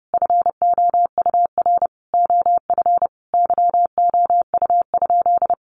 Morse code, heard as a keyed tone, about 30 wpm, 700 Hz. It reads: FOUR OF YOU?
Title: Sentences from Top 300 Words 30wpm